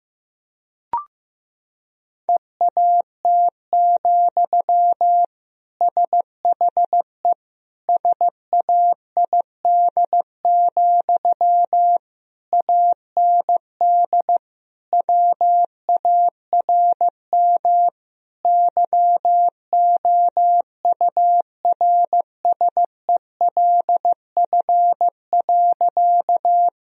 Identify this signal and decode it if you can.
Morse code, heard as a keyed tone, about 15 wpm, 700 Hz. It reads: EAT, SHE SAID, AND WARM YOURSELF.
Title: Sentences from Top 1000 Words - Review Part 2 - 15wpm